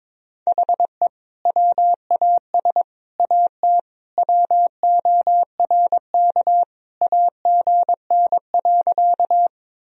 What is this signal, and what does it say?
Morse code, heard as a keyed tone, about 22 wpm, 700 Hz. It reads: HE WAS AT WORK AGN.
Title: Sentences from Top 200 Words and Abbreviations 22wpm